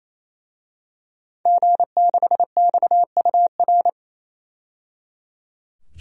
Morse code, heard as a keyed tone, about 28 wpm, 700 Hz. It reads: G6XUR
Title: DXCC-100 Call Signs - 5 Characters - 28wpm